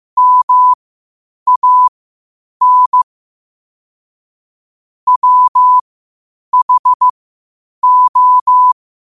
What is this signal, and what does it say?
Morse code, heard as a keyed tone, about 15 wpm, spaced as for 8 wpm, 1000 Hz. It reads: MAN WHO
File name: qotdfiles / 8wpm